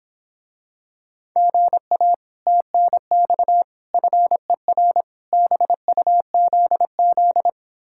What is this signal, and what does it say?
Morse code, heard as a keyed tone, about 26 wpm, 700 Hz. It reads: GA TNX FER BUZZ